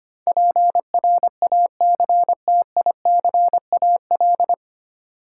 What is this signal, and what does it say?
Morse code, heard as a keyed tone, about 25 wpm, 700 Hz. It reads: PRACTICAL
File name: practice / 25wpm / 9 Letter Words - First 3 Letters Given - 25wpm